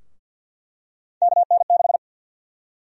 Morse code, heard as a keyed tone, about 52 wpm, 700 Hz. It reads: KN6
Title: USA Call Sign Prefixes - Repeated 2x - 50wpm